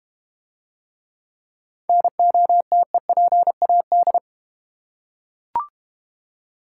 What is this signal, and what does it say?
Morse code, heard as a keyed tone, about 32 wpm, 700 Hz. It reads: NOTEPAD